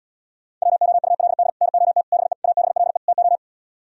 Morse code, heard as a keyed tone, about 77 wpm, 700 Hz. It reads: WOULD NOT BE ABLE TO